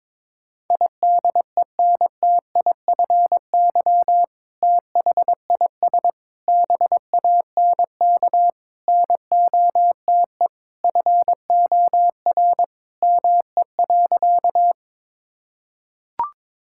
Morse code, heard as a keyed tone, about 21 wpm, 700 Hz. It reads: IDENTIFY THIS BANK NOTE FOR ME.
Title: Sentences from Top 1000 Words - Review Part 6 - 22wpm